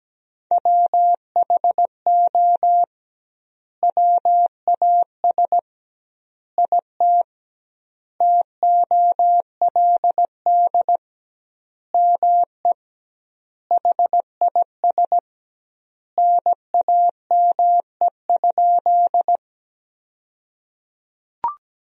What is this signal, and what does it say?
Morse code, heard as a keyed tone, about 17 wpm, 700 Hz. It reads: WHO WAS IT TOLD ME HIS NAME?
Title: 2x Word-Spacing - Sentences from Top 400 Words - 17wpm